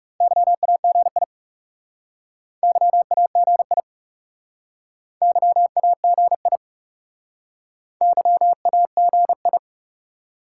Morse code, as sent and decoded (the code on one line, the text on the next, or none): -.-- .- --. .. / -.-- .- --. .. / -.-- .- --. .. / -.-- .- --. ..
YAGI YAGI YAGI YAGI